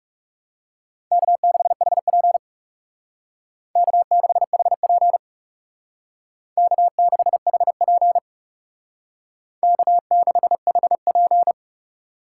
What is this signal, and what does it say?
Morse code, heard as a keyed tone, about 36 wpm, 700 Hz. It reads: K6HP K6HP K6HP K6HP
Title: Speed Racing - USA 1x2 Call Signs 30wpm